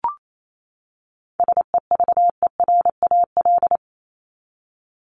Morse code, heard as a keyed tone, about 28 wpm, 700 Hz. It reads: SEVERAL
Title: Top 500 Words 28wpm